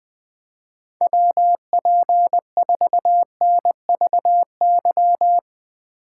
Morse code, heard as a keyed tone, about 21 wpm, 700 Hz. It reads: WP4NVY
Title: Speed Racing - USA 2x3 Call Signs 20wpm